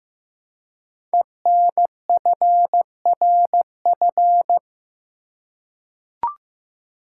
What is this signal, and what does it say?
Morse code, heard as a keyed tone, about 15 wpm, 700 Hz. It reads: ENFRF
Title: Code Groups - 5 Letters - 15wpm